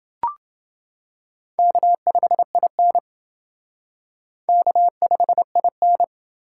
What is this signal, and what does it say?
Morse code, heard as a keyed tone, about 28 wpm, 700 Hz. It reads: K5IN K5IN